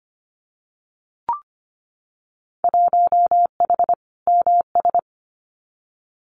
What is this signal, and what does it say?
Morse code, heard as a keyed tone, about 25 wpm, 700 Hz. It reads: 1H MS
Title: Winter Field Day Exchanges 25wpm